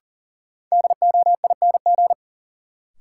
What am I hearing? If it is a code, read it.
Morse code, heard as a keyed tone, about 40 wpm, 700 Hz. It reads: DOING
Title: Sets of 4 Words - New Word Drill - 40wpm